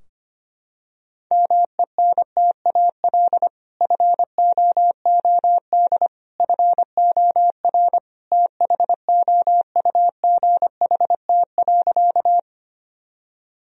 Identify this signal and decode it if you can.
Morse code, heard as a keyed tone, about 25 wpm, 700 Hz. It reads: MENTAL FOOD FOR THOUGHT.
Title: Sentences from Top 1000 Words - Review Part 4 - 25wpm